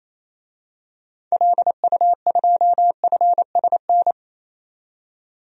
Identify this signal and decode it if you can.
Morse code, heard as a keyed tone, about 28 wpm, 700 Hz. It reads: LU2FSN
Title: DXCC-100 Call Signs - 6 Characters - 28wpm